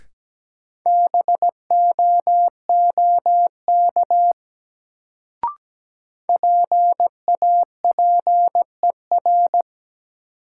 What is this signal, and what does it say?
Morse code, heard as a keyed tone, about 17 wpm, 700 Hz. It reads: BOOK PAPER
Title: Sentences from Top 300 Words - New Word Drill - 17wpm